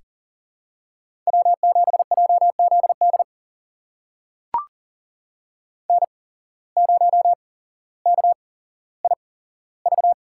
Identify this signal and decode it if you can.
Morse code, heard as a keyed tone, about 40 wpm, spaced as for 17 wpm, 700 Hz. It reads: W7JZD N0KIV
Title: USA Call Signs - 5 Characters - 8x Character Spacing - 40wpm